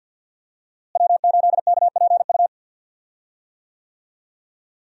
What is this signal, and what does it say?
Morse code, heard as a keyed tone, about 48 wpm, 700 Hz. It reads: W8KPU
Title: USA Call Signs - 5 Characters - Repeated 2x - 50wpm